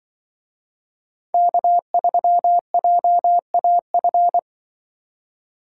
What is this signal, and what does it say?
Morse code, heard as a keyed tone, about 25 wpm, 700 Hz. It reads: K3JAF
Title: Speed Racing - USA Call Signs 20wpm